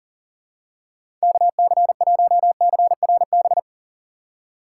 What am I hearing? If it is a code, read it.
Morse code, heard as a keyed tone, about 38 wpm, 700 Hz. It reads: KC1CRB
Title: USA Call Signs - 6 Characters - 2x Character Spacing - 40wpm